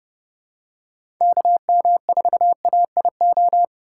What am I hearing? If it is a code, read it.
Morse code, heard as a keyed tone, about 30 wpm, 700 Hz. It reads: KM4AIO
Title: USA Call Signs - 6 Characters - 3x Character Spacing - 30wpm